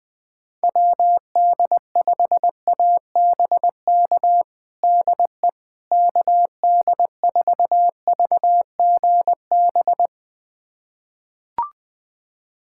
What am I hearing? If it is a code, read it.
Morse code, heard as a keyed tone, about 20 wpm, 700 Hz. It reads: WD5ABK DE KD4VGB
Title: USA Callsign DE Callsign 20wpm